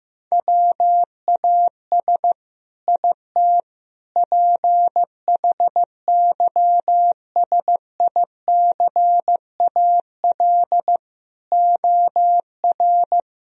Morse code, heard as a keyed tone, about 15 wpm, 700 Hz. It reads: WAS IT PHYSICAL OR
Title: Sentences from Top 1000 Words - Review Part 4 - 15wpm